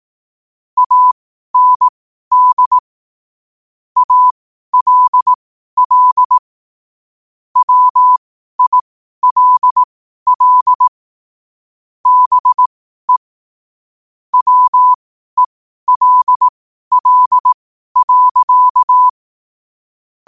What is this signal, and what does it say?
Morse code, heard as a keyed tone, about 18 wpm, spaced as for 12 wpm, 1000 Hz. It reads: AND ALL WILL BE WELL.
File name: qotdfiles / 12wpm